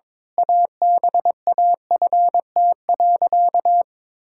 Morse code, heard as a keyed tone, about 22 wpm, 700 Hz. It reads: ABAFT.
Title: Pangram Challenge v2 - 22wpm